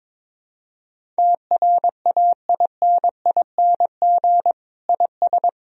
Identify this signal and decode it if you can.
Morse code, heard as a keyed tone, about 22 wpm, 700 Hz. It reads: TRAINING IS
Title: Sentences from Top 1000 Words - Review Part 4 - 22wpm